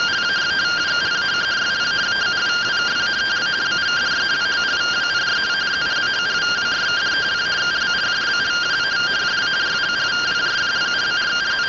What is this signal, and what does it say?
Two-tone teleprinter signal, about 170 Hz shift.